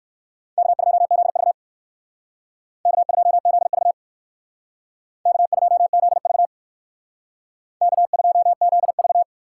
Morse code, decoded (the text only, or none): K2ZV K2ZV K2ZV K2ZV